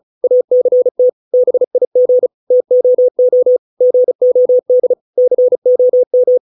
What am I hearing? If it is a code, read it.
Morse code, heard as a keyed tone, about 36 wpm, 500 Hz. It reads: ACT BIG TOO GOD COM